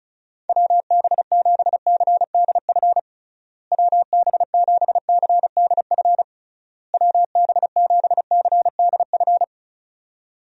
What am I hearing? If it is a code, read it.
Morse code, heard as a keyed tone, about 34 wpm, 700 Hz. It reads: WB7CDF WB7CDF WB7CDF
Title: USA Call Signs - 6 Characters - Repeated 3x - 35wpm